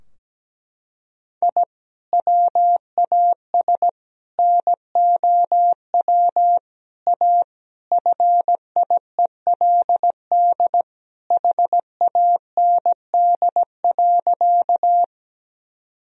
Morse code, heard as a keyed tone, about 17 wpm, 700 Hz. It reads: I WAS NOW A FIELD HAND.
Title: Sentences from Top 1000 Words - Review Part 2 - 17wpm